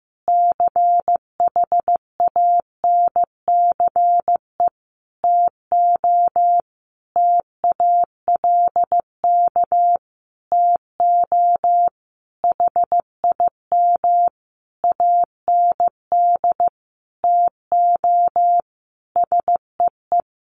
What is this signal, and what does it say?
Morse code, heard as a keyed tone, about 15 wpm, 700 Hz. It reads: CHANCE TO TALK TO HIM AND TO SEE
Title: Sentences from Top 1400 Words 15wpm